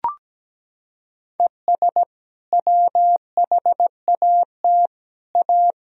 Morse code, heard as a keyed tone, about 17 wpm, 700 Hz. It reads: ES WHAT A
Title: Sentences from Top 500 Words and Abbreviations 17wpm